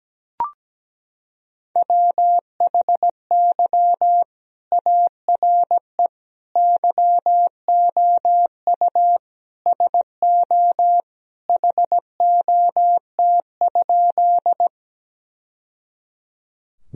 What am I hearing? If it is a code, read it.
Morse code, heard as a keyed tone, about 17 wpm, 700 Hz. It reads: WHY ARE YOU SO HOT?
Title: Sentences from Top 1000 Words - Review Part 3 - 17wpm